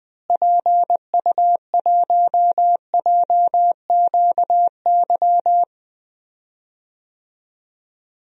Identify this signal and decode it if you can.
Morse code, heard as a keyed tone, about 20 wpm, 700 Hz. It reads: PU1JQY
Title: DXCC-100 Call Signs - 6 Characters - Repeated 3x - 20wpm